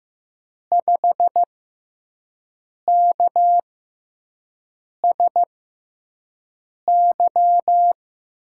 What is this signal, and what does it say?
Morse code, heard as a keyed tone, about 15 wpm, spaced as for 5 wpm, 700 Hz. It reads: 5KSY
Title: USA Call Signs - 6 Characters - 6x Character Spacing - 15wpm